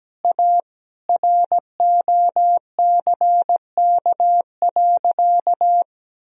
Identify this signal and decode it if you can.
Morse code, heard as a keyed tone, about 17 wpm, 700 Hz. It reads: A ROCK.
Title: Sentences from Top 1000 Words - Review Part 2 - 17wpm